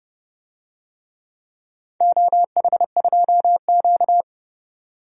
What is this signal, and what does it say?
Morse code, heard as a keyed tone, about 32 wpm, 700 Hz. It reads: OH2Q
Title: DXCC-100 Call Signs - 4 Characters - 30wpm